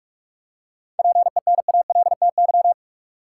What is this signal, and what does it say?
Morse code, heard as a keyed tone, about 44 wpm, 700 Hz. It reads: PENALTY
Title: Seven Letter Words - 45wpm